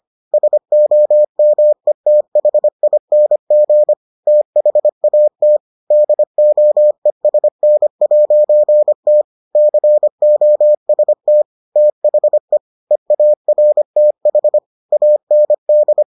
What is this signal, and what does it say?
Morse code, heard as a keyed tone, about 25 wpm, 600 Hz. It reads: SOMETHING THAT DOESN'T COST THE EARTH AND